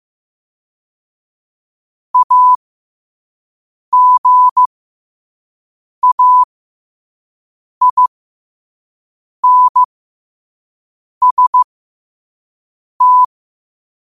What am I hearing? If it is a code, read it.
Morse code, heard as a keyed tone, about 15 wpm, spaced as for 5 wpm, 1000 Hz. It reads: AGAINST
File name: qotdfiles / 5wpm